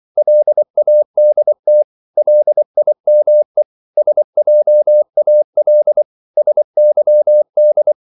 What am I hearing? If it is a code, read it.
Morse code, heard as a keyed tone, about 24 wpm, 600 Hz. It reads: LADT LIME SJAL SYD
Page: Korte danske ord 24wpm